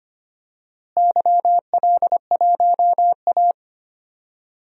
Morse code, heard as a keyed tone, about 25 wpm, 700 Hz. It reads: YL1A
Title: DXCC-100 Call Signs - 4 Characters - Repeated 2x - 25wpm